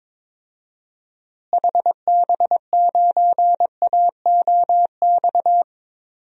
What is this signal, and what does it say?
Morse code, heard as a keyed tone, about 22 wpm, 700 Hz. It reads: HB9AOX
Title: DXCC-100 Call Signs - 6 Characters - Repeated 2x - 22wpm